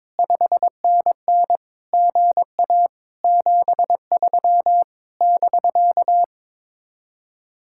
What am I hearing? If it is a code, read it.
Morse code, heard as a keyed tone, about 22 wpm, 700 Hz. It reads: 5NN GA 73 <BK>
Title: Speed-Racing Common QSO Phrases - 22wpm